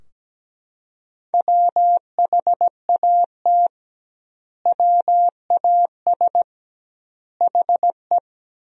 Morse code, heard as a keyed tone, about 17 wpm, 700 Hz. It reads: WHAT WAS HE
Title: 2x Word-Spacing - Sentences from Top 100 Words - 17wpm